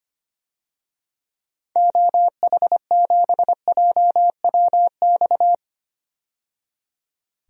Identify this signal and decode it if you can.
Morse code, heard as a keyed tone, about 25 wpm, 700 Hz. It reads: OH7JWX
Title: DXCC-100 Call Signs - 6 Characters - 25wpm